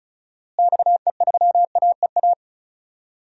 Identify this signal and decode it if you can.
Morse code, heard as a keyed tone, about 36 wpm, 700 Hz. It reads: XE3AEA
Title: DXCC-100 Call Signs - 6 Characters - Repeated 3x - 35wpm